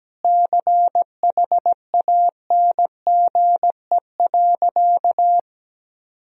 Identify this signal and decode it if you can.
Morse code, heard as a keyed tone, about 17 wpm, 700 Hz. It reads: CHANGE.
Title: Sentences from Top 1000 Words - Review Part 1 - 17wpm